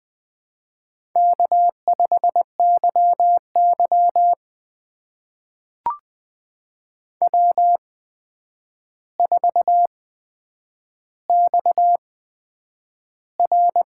Morse code, heard as a keyed tone, about 20 wpm, spaced as for 9 wpm, 700 Hz. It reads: K5YY W4XR
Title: USA Call Signs - 4 Characters - 8x Character Spacing - 20wpm